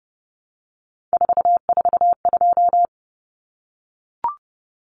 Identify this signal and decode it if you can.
Morse code, heard as a keyed tone, about 30 wpm, 700 Hz. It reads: 442
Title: Numbers - 3-Digit 30wpm